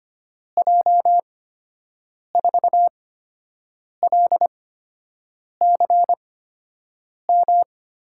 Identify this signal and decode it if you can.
Morse code, heard as a keyed tone, about 25 wpm, spaced as for 7 wpm, 700 Hz. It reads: J4LCM
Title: USA Call Signs - 6 Characters - 8x Character Spacing - 25wpm